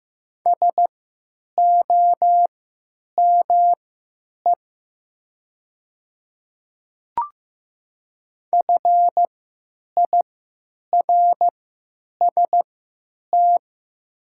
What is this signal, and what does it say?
Morse code, heard as a keyed tone, about 15 wpm, spaced as for 9 wpm, 700 Hz. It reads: SOME FIRST
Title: 3x Character Spacing - Top 100 Words - 15wpm